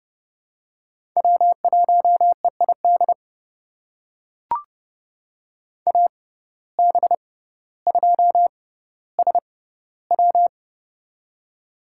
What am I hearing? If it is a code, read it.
Morse code, heard as a keyed tone, about 31 wpm, spaced as for 15 wpm, 700 Hz. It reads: W1EID AB2SW
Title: USA Call Signs - 6x Character Spacing - 30wpm